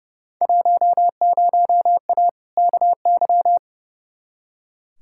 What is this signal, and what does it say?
Morse code, heard as a keyed tone, about 30 wpm, 700 Hz. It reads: 10A KY
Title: Field Day Exchanges Encore - 30wpm